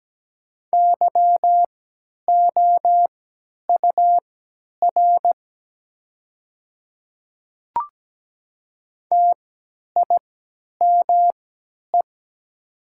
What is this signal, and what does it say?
Morse code, heard as a keyed tone, about 17 wpm, spaced as for 10 wpm, 690 Hz. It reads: YOUR TIME
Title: 3x Character Spacing - Top 300 Words - 17wpm